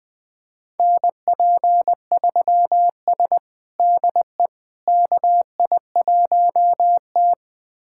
Morse code, heard as a keyed tone, about 20 wpm, 700 Hz. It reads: NP3S DE KI1T
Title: USA Callsign DE Callsign - 4 Characters - 20wpm